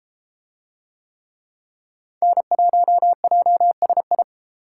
Morse code, heard as a keyed tone, about 32 wpm, 700 Hz. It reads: N1JSI